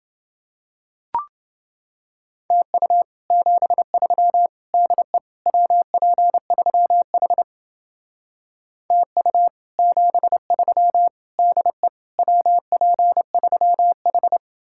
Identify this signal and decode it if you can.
Morse code, heard as a keyed tone, about 30 wpm, 700 Hz. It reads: TU 73 DE WP3H TU 73 DE WP3H